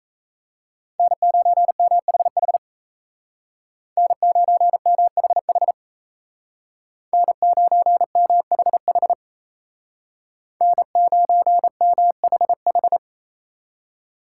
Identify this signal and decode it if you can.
Morse code, heard as a keyed tone, about 36 wpm, 700 Hz. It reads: N9MHH N9MHH N9MHH N9MHH